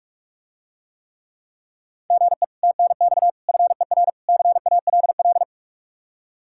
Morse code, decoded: GE TNX FER CALL

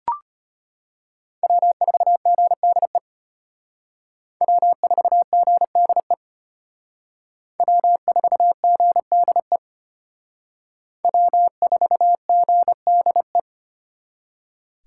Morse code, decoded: W4GDE W4GDE W4GDE W4GDE

